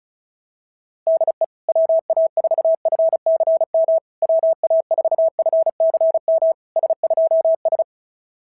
Morse code, heard as a keyed tone, about 36 wpm, 690 Hz. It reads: DE WA4FCM WA4FCM S2S
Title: SOTA QSO Practice 35wpm